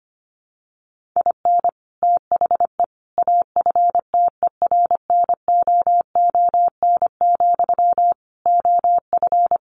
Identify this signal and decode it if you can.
Morse code, heard as a keyed tone, about 25 wpm, 700 Hz. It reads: IN THE AFTERNOON, OF
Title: Sentences from Top 1500 Words 25wpm